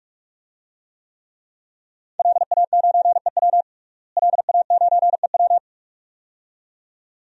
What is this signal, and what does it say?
Morse code, heard as a keyed tone, about 43 wpm, 700 Hz. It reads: LA9EW LA9EW
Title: DXCC-100 Call Signs - 5 Characters - Repeated 2x - 45wpm